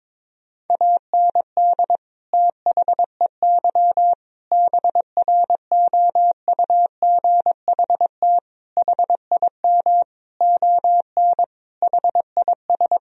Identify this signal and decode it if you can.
Morse code, heard as a keyed tone, about 22 wpm, 700 Hz. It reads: AND THEY BROUGHT HIM ON HIS
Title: Sentences from Top 1000 Words - Review Part 2 - 22wpm